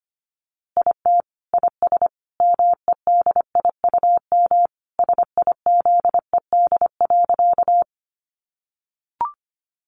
Morse code, heard as a keyed tone, about 25 wpm, 700 Hz. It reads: IT IS MEDIUM SIZED.
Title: Easy Phrases 25wpm